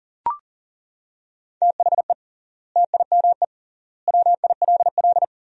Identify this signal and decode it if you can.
Morse code, heard as a keyed tone, about 40 wpm, 700 Hz. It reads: THE TIME WILL
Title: 3x Word-Spacing - Sentences from Top 100 Words - 40wpm